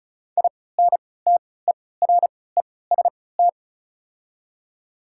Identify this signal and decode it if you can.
Morse code, heard as a keyed tone, about 36 wpm, spaced as for 20 wpm, 690 Hz. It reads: INTEREST
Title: 3x Character Spacing - Top 500 Words - 35wpm